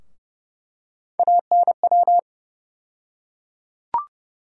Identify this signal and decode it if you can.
Morse code, heard as a keyed tone, about 30 wpm, 700 Hz. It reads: ANW
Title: USA Call Sign Postfixes - 3 Letters 30wpm